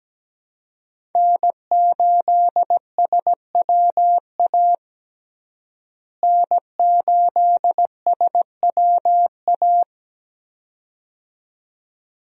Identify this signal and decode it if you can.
Morse code, heard as a keyed tone, about 17 wpm, 700 Hz. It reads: N8SWA N8SWA
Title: USA Call Signs - 5 Characters - Repeated 2x - 17wpm